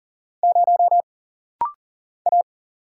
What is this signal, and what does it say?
Morse code, heard as a keyed tone, about 40 wpm, 700 Hz. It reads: 0 A
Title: Single Letter-Number Alt - Rapid-Fire 40wpm